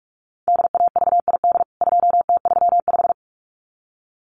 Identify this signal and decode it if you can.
Morse code, heard as a keyed tone, about 44 wpm, 700 Hz. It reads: DAVID 2T35
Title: CWT Exchange 45wpm